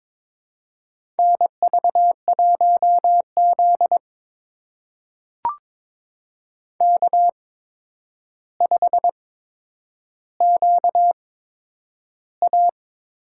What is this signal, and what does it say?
Morse code, heard as a keyed tone, about 22 wpm, spaced as for 9 wpm, 700 Hz. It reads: NV1Z K5QA
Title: USA Call Signs - 8x Character Spacing - 22wpm